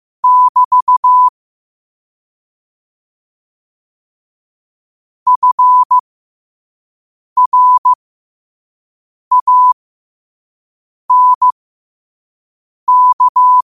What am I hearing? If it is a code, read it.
Morse code, heard as a keyed tone, about 15 wpm, spaced as for 5 wpm, 1000 Hz. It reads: <BT> FRANK
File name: qotdfiles / 5wpm